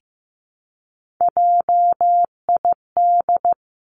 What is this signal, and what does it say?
Morse code, heard as a keyed tone, about 15 wpm, 700 Hz. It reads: JID
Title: Three Challenging Letter-Number Combinations 15wpm